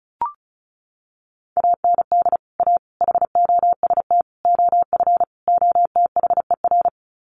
Morse code, heard as a keyed tone, about 34 wpm, 700 Hz. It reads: AND A HOST OF OTHER
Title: Sets of 5 Words Encore2 35wpm